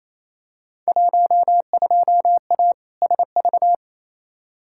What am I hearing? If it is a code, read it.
Morse code, heard as a keyed tone, about 28 wpm, 700 Hz. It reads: 12A SV
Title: Field Day Exchanges Encore - 28wpm